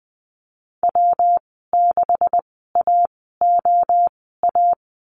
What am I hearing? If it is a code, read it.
Morse code, heard as a keyed tone, about 19 wpm, spaced as for 14 wpm, 700 Hz. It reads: W6AOA
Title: 140 Lesson 23g - Letter f - Call Signs 20wpm